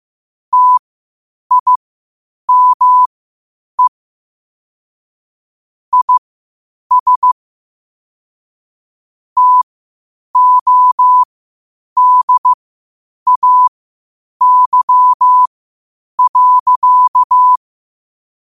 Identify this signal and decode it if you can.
Morse code, heard as a keyed tone, about 15 wpm, spaced as for 8 wpm, 1000 Hz. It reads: TIME IS TODAY.
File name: qotdfiles / 8wpm